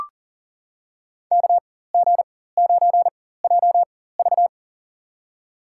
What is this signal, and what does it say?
Morse code, heard as a keyed tone, about 40 wpm, spaced as for 19 wpm, 700 Hz. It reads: KG9JV
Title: USA Call Signs - 5 Characters - 4x Character Spacing - 40wpm